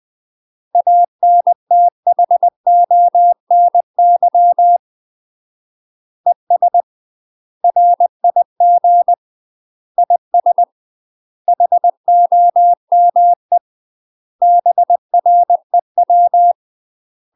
Morse code, heard as a keyed tone, about 20 wpm, 700 Hz. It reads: ANTHONY ES RIG IS HOME BREW
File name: UK MASTER QSOs TRAINING / 20WPM / UK MASTER QSOs 0038